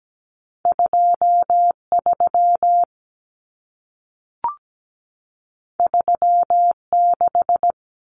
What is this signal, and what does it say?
Morse code, heard as a keyed tone, about 17 wpm, 700 Hz. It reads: 23 36
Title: Numbers - 2-Digit 17wpm